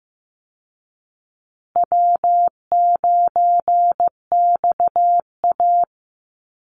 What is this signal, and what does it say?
Morse code, heard as a keyed tone, about 15 wpm, 700 Hz. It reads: W9XA
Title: Call Signs Repeated 2x - 15wpm